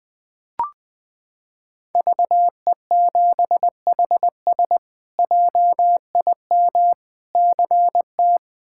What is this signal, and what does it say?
Morse code, heard as a keyed tone, about 20 wpm, 700 Hz. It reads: VE7HS JIM CT
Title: K1USN SST QSO Practice - 20wpm